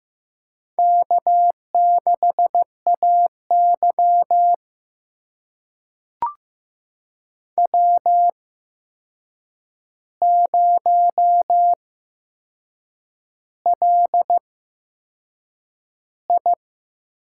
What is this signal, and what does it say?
Morse code, heard as a keyed tone, about 15 wpm, spaced as for 6 wpm, 700 Hz. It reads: K6AY W0LI
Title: USA Call Signs - 4 Characters - 8x Character Spacing - 15wpm